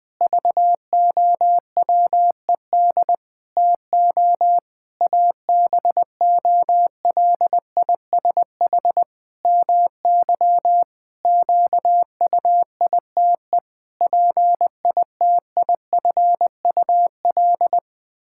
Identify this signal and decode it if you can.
Morse code, heard as a keyed tone, about 20 wpm, 700 Hz. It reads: VOWED TO ABOLISH MY QUITE PITIFUL
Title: Pangram Challenge v2 - 20wpm